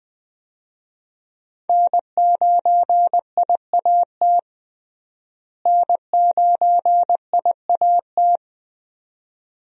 Morse code, heard as a keyed tone, about 20 wpm, 700 Hz. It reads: N9IAT N9IAT
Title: USA Call Signs - 5 Characters - Repeated 2x - 20wpm